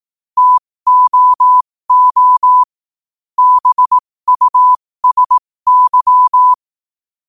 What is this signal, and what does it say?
Morse code, heard as a keyed tone, about 18 wpm, spaced as for 15 wpm, 1000 Hz. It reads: TOO BUSY
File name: qotdfiles / 15wpm